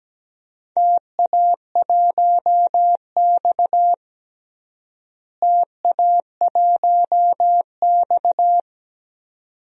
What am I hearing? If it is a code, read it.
Morse code, heard as a keyed tone, about 17 wpm, 700 Hz. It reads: TA1X TA1X
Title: DXCC-100 Call Signs - 4 Characters - Repeated 2x - 17wpm